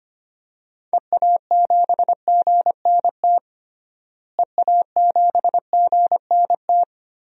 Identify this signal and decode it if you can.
Morse code, heard as a keyed tone, about 25 wpm, 700 Hz. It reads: EA7GNT EA7GNT
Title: DXCC-100 Call Signs - 6 Characters - Repeated 3x - 25wpm